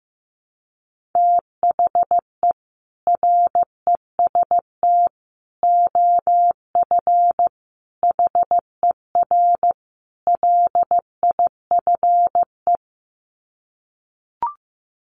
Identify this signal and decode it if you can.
Morse code, heard as a keyed tone, about 15 wpm, 700 Hz. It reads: THE REST OF HER LIFE
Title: Sets of 5 Words Encore 15wpm